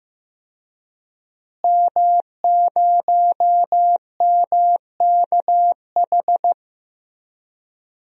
Morse code, heard as a keyed tone, about 15 wpm, 700 Hz. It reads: M0MKH
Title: DXCC-100 Call Signs - 5 Characters - Repeated 2x - 15wpm